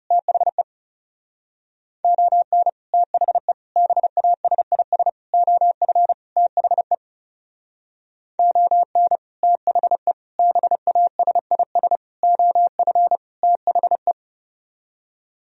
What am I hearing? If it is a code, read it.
Morse code, heard as a keyed tone, about 36 wpm, 700 Hz. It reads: THE ON THE BASIS OF THE ON THE BASIS OF THE
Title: Speed-Racing - Sets of 5 Words 30wpm